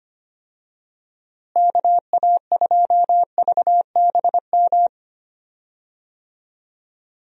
Morse code, heard as a keyed tone, about 25 wpm, 700 Hz. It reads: KA2VBM
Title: USA Call Signs - 6 Characters - Repeated 2x - 25wpm